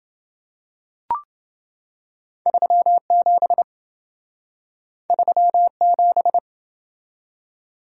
Morse code, heard as a keyed tone, about 27 wpm, 700 Hz. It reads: 37 37